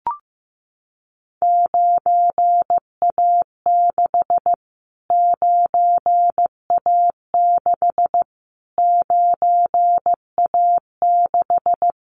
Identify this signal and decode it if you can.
Morse code, heard as a keyed tone, about 15 wpm, 700 Hz. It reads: 9A6 9A6 9A6
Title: Callsign Prefixes for Top 25 DXCC - Repeated 3x - 15wpm